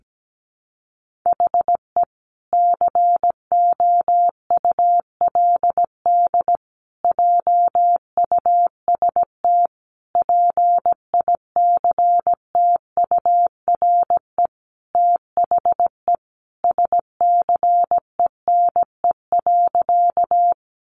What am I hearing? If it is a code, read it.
Morse code, heard as a keyed tone, about 17 wpm, 700 Hz. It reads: HE COULD JUST PICTURE THE SCENE.
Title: Sentences from Top 1100 Words 17wpm